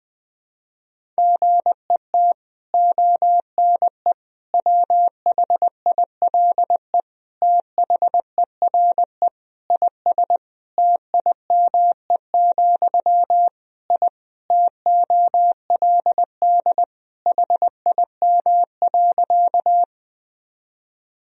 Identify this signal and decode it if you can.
Morse code, heard as a keyed tone, about 21 wpm, 700 Hz. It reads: GET ONE WHILE THERE IS TIME, I TOLD HIM.
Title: Sentences from Top 1000 Words - Review Part 1 - 20wpm